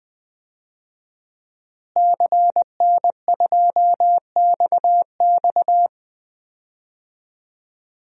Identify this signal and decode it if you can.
Morse code, heard as a keyed tone, about 20 wpm, 700 Hz. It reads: CN2XX
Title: DXCC-100 Call Signs - 5 Characters - Repeated 3x - 20wpm